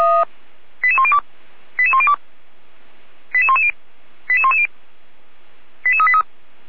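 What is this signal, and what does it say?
Five-tone selective call (ZVEI), heard as quick runs of five tones, tones of about 70 ms. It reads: 88102 88102 88110 88110 88383
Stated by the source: Five-tone ZVEI selective call, received by SDR.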